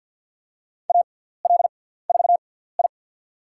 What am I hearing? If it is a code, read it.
Morse code, heard as a keyed tone, about 49 wpm, spaced as for 17 wpm, 700 Hz. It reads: AL4I
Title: USA Call Signs - 6x Character Spacing - 50wpm